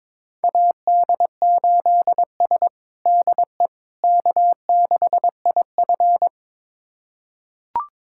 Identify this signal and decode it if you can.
Morse code, heard as a keyed tone, about 22 wpm, 700 Hz. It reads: AD8S DE K6IF
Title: USA Callsign DE Callsign - 4 Characters - 22wpm